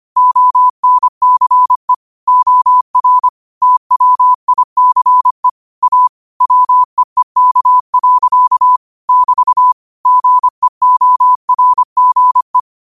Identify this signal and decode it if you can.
Morse code, heard as a keyed tone, about 25 wpm, 1000 Hz. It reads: ONCE OR TWICE A WEEK. <BT> GEORGE
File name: qotdfiles / 25wpm